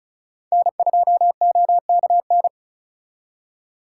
Morse code, heard as a keyed tone, about 47 wpm, 700 Hz. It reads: N2OKN